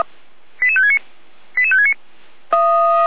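Five-tone selective call (ZVEI), heard as quick runs of five tones, tones of about 80 ms. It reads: 88569 88569
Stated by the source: ZVEI five-tone selective call received on an SDR.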